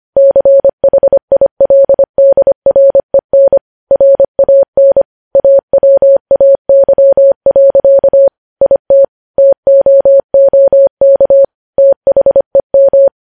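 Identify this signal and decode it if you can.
Morse code, heard as a keyed tone, about 25 wpm, 550 Hz. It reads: CHILDREN RAN AWAY. IT TOOK THEM
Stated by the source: Morse news headlines sent at 25 wpm.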